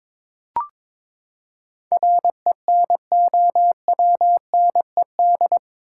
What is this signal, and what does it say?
Morse code, heard as a keyed tone, about 22 wpm, 700 Hz. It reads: RENOWNED
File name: practice / 22wpm / Eight Letter Words - 22wpm